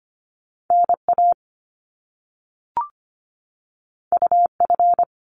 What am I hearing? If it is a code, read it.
Morse code, heard as a keyed tone, about 25 wpm, 700 Hz. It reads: NA UF
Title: Top Two Letter Combinations in Words 25wpm